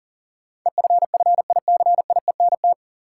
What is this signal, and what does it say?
Morse code, heard as a keyed tone, about 40 wpm, 700 Hz. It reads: EFFICIENT
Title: Sentences from Top 1300 Words - New Word Drill - 40wpm